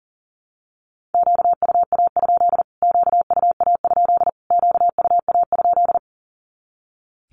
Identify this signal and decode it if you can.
Morse code, heard as a keyed tone, about 40 wpm, 700 Hz. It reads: QUA? QUA? QUA?
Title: Q-Codes Repeated 3x 40wpm